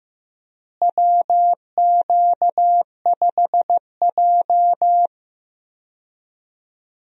Morse code, heard as a keyed tone, about 15 wpm, 700 Hz. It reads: WQ5J
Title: USA Call Signs - 4 Characters - Repeated 2x - 15wpm